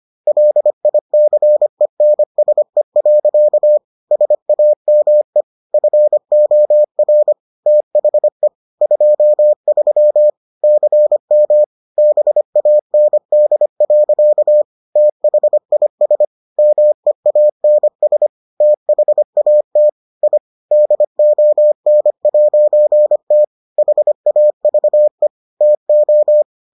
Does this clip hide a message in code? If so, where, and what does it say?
Yes: Morse code from 0.3 to 26.4 s, heard as a keyed tone, about 25 wpm, 600 Hz. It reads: LICENSE. SAME FOR THE 23 CM BAND. THIS MEANS THAT I DON'T HAVE TO